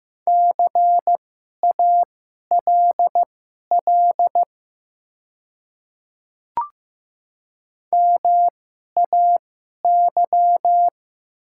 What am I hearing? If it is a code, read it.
Morse code, heard as a keyed tone, about 15 wpm, spaced as for 11 wpm, 700 Hz. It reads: CALL MAY
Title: 2x Character Spacing - Top 100 Words - 15wpm